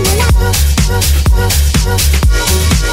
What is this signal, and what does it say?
There is no code.